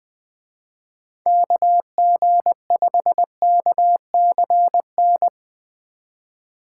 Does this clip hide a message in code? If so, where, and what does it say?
Yes: Morse code from 1.3 to 5.3 s, heard as a keyed tone, about 20 wpm, 700 Hz. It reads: KG5KCN